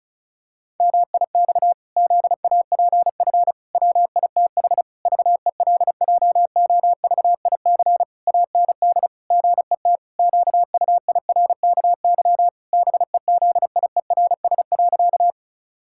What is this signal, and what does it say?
Morse code, heard as a keyed tone, about 36 wpm, 700 Hz. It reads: MIX ZAPF WITH VELJOVIC AND GET QUIRKY BEZIERS.
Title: Pangram Challenge v2 - 35wpm